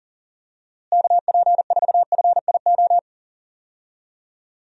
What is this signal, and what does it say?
Morse code, heard as a keyed tone, about 40 wpm, 700 Hz. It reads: KP4FIO
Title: USA Call Signs - 6 Characters - Repeated 3x - 40wpm